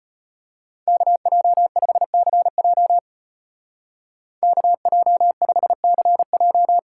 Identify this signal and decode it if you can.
Morse code, heard as a keyed tone, about 30 wpm, 700 Hz. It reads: KJ5CJ KJ5CJ
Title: Speed Racing - USA Call Signs 25wpm